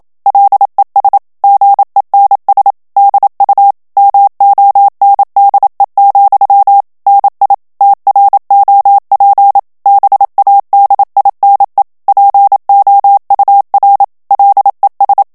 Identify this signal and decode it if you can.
Morse code, heard as a keyed tone, about 27 wpm, 800 Hz. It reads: LES GENS DU MONDE, NI TROP BADINE POUR LES
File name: FRANCAIS TEXTE CLAIR 2 VITESSE 26 MOTS MINUTES